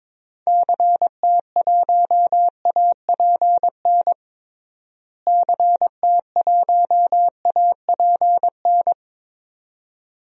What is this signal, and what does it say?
Morse code, heard as a keyed tone, about 22 wpm, 700 Hz. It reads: CT1APN CT1APN
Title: DXCC-100 Call Signs - 6 Characters - Repeated 2x - 22wpm